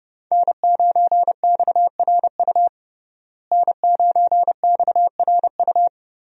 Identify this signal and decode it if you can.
Morse code, heard as a keyed tone, about 30 wpm, 700 Hz. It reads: N9XRU N9XRU
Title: USA Call Signs - 5 Characters - Repeated 3x - 30wpm